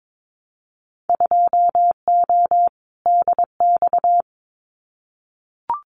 Morse code, heard as a keyed tone, about 22 wpm, 700 Hz. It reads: 2O DX
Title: Winter Field Day Exchanges Encore 22wpm